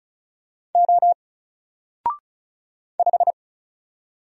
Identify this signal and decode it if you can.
Morse code, heard as a keyed tone, about 34 wpm, 700 Hz. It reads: O 5
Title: Single Letter-Number Alt 35wpm